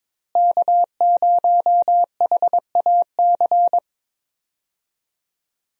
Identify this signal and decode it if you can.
Morse code, heard as a keyed tone, about 22 wpm, 700 Hz. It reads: K0HAC
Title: USA Call Signs - 5 Characters - Repeated 3x - 22wpm